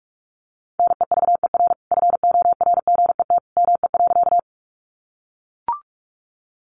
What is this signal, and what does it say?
Morse code, heard as a keyed tone, about 44 wpm, 700 Hz. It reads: NEVER FORGET ME.
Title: Sentences from Top 1200 Words 45wpm